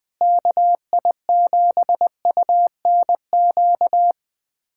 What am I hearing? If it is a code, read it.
Morse code, heard as a keyed tone, about 20 wpm, 700 Hz. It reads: KI7UNQ
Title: USA Callsign DE Callsign - 6 Characters - 20wpm